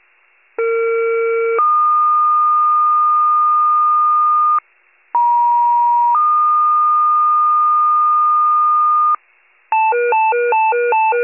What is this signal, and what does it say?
Two-tone page, heard as a one-second tone, then a three-second tone, about 460 then 1220 Hz, about 950 then 1220 Hz. It is heard from 0.6 to 9.2 s.